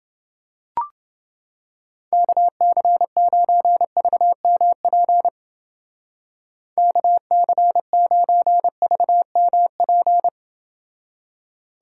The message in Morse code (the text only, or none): KC9VMP KC9VMP